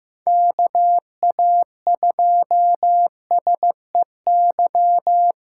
Morse code, heard as a keyed tone, about 15 wpm, 700 Hz. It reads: KA2SEY
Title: USA Callsign DE Callsign 15wpm